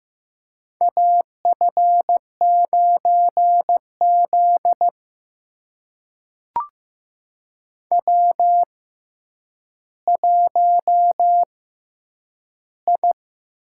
Morse code, heard as a keyed tone, about 15 wpm, spaced as for 9 wpm, 700 Hz. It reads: AF9Z W1I
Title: USA Call Signs - 4 Characters - 6x Character Spacing - 15wpm